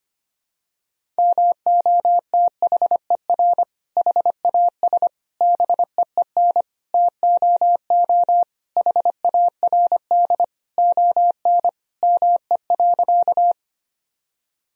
Morse code, heard as a keyed tone, about 25 wpm, 700 Hz. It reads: MOTHER HAS BEEN TOO HARD ON ME.
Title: Sentences from Top 1000 Words - Review Part 1 - 25wpm